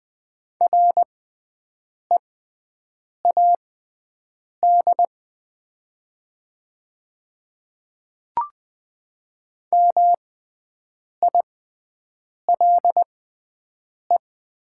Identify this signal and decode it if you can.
Morse code, heard as a keyed tone, about 20 wpm, spaced as for 7 wpm, 700 Hz. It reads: READ MILE
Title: 6x Character Spacing - Top 300 Words - 20wpm